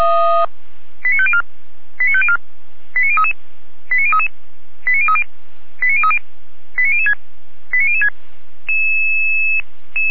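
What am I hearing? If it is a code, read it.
Five-tone selective call (ZVEI), heard as quick runs of five tones, tones of about 70 ms. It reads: 89504 89504 89033 89033 89030 89030 89006 89006
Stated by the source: ZVEI five-tone selective call 89099 received on SDR, fire brigade (Feuerwehr Orlamünde) recording.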